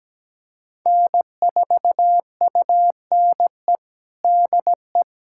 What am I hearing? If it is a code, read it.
Morse code, heard as a keyed tone, about 17 wpm, 700 Hz. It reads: N4UNE DE
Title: USA Callsign DE Callsign - 5 Characters - 17wpm